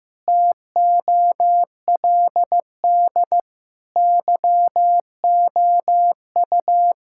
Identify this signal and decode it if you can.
Morse code, heard as a keyed tone, about 16 wpm, 700 Hz. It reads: TOLD YOU
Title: Sentences from Top 1000 Words - Review Part 6 - 15wpm